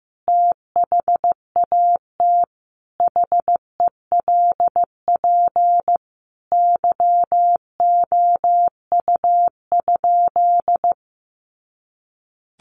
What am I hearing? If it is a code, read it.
Morse code, heard as a keyed tone, about 15 wpm, 700 Hz. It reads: THAT HELP YOU?
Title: Sentences from Top 200 Words 15wpm